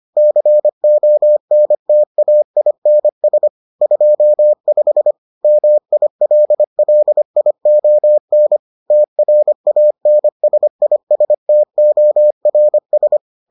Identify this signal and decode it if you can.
Morse code, heard as a keyed tone, about 25 wpm, 600 Hz. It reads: CONTAINS 25 MILLION TRANSISTORS